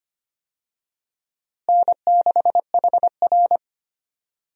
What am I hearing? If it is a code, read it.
Morse code, heard as a keyed tone, about 25 wpm, 700 Hz. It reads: N6HR
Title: USA Call Signs - 4 Characters - Repeated 2x - 25wpm